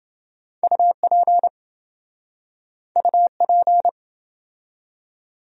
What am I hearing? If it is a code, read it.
Morse code, heard as a keyed tone, about 27 wpm, 700 Hz. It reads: UP UP